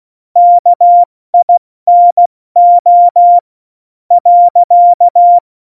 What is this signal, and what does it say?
Morse code, heard as a keyed tone, about 16 wpm, 700 Hz. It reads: KINO .
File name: Morsekurs / Witze / Witz 19. Juni 2018, 70 F80